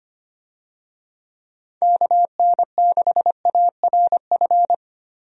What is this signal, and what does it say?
Morse code, heard as a keyed tone, about 25 wpm, 700 Hz. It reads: KN6ARF
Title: USA Call Signs - 6 Characters - Repeated 3x - 25wpm